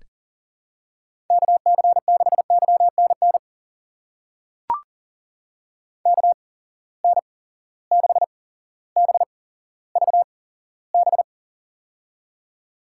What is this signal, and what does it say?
Morse code, heard as a keyed tone, about 38 wpm, spaced as for 17 wpm, 700 Hz. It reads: KC6YNN KN6BVB
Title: USA Call Signs - 6 Characters - 8x Character Spacing - 40wpm